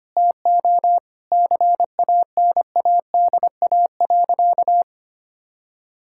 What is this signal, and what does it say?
Morse code, heard as a keyed tone, about 25 wpm, 700 Hz. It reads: TO CANADA.
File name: practice / 25wpm / Sentences with 4-Word Phrases - 25wpm